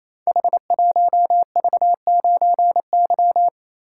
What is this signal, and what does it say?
Morse code, heard as a keyed tone, about 27 wpm, 700 Hz. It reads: H1V9Y
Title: Code Groups - 5 Letter:Numbers - 28wpm